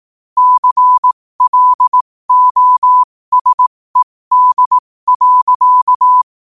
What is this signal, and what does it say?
Morse code, heard as a keyed tone, about 18 wpm, spaced as for 15 wpm, 1000 Hz. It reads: CLOSED.